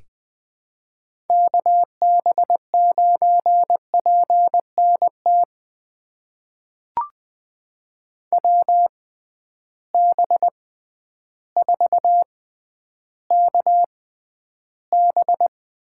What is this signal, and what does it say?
Morse code, heard as a keyed tone, about 20 wpm, spaced as for 11 wpm, 700 Hz. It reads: KB9PNT WB4KB
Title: USA Call Signs - 6 Characters - 6x Character Spacing - 20wpm